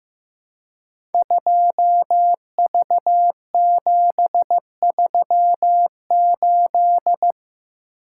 Morse code, heard as a keyed tone, about 15 wpm, 700 Hz. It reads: 2V738
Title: Code Groups - 5 Letter:Numbers - 15wpm